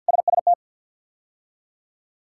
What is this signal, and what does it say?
Morse code, heard as a keyed tone, about 51 wpm, 700 Hz. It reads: SST
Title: K1USN SST QSO Practice - 50wpm